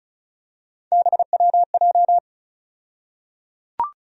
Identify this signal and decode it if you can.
Morse code, heard as a keyed tone, about 34 wpm, 700 Hz. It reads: BWJ
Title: USA Call Sign Postfixes - 3 Letters 35wpm